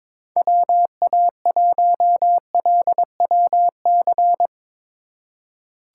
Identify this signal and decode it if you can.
Morse code, heard as a keyed tone, about 22 wpm, 700 Hz. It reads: WA1LWC
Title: K1USN SST QSO Practice - 22wpm